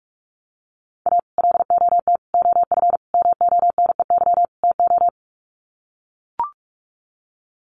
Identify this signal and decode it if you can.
Morse code, heard as a keyed tone, about 44 wpm, 700 Hz. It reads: A LOT OF MONEY TO
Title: Sets of 5 Words Encore2 45wpm